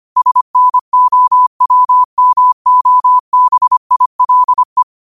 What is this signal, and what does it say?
Morse code, heard as a keyed tone, about 25 wpm, 1000 Hz. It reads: SNOWMOBILE
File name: 25wpm